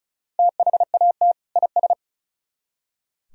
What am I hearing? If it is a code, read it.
Morse code, heard as a keyed tone, about 45 wpm, 700 Hz. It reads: THAT IS